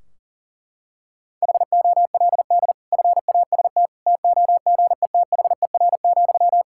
Morse code, heard as a keyed tone, about 40 wpm, 700 Hz. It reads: HOLD FAST TOGETHER,
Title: Sentences from Top 1600 Words 40wpm